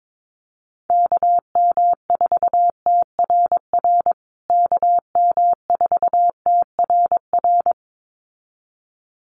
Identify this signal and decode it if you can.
Morse code, heard as a keyed tone, about 22 wpm, 700 Hz. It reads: KM4TRR KM4TRR
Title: USA Call Signs Repeated 2x 22wpm